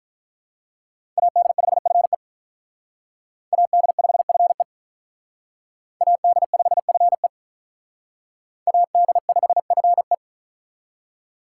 Morse code, heard as a keyed tone, about 46 wpm, 700 Hz. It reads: AD5FE AD5FE AD5FE AD5FE